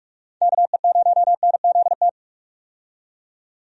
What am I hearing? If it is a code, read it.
Morse code, heard as a keyed tone, about 48 wpm, 700 Hz. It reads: KE0NZT
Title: USA Call Signs - 6 Characters - 45wpm